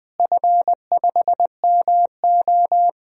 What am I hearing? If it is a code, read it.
Morse code, heard as a keyed tone, about 20 wpm, 700 Hz. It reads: F5MO
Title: DXCC-100 Call Signs - 4 Characters - Repeated 2x - 20wpm